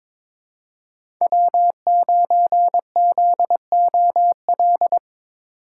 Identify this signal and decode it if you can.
Morse code, heard as a keyed tone, about 22 wpm, 700 Hz. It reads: W9ZOL
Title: USA Call Signs - 5 Characters - 4x Character Spacing - 22wpm